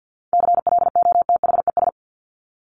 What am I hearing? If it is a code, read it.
Morse code, heard as a keyed tone, about 50 wpm, 700 Hz. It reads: CLOTHES
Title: Top 1751-2000 Words 50wpm